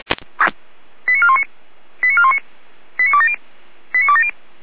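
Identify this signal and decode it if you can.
Five-tone selective call (ZVEI), heard as quick runs of five tones, tones of about 80 ms. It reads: 89419 89419 89270 89270
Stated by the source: ZVEI five-tone call 89508 received on SDR, from the Feuerwehr Orlamünde (fire brigade) audio server.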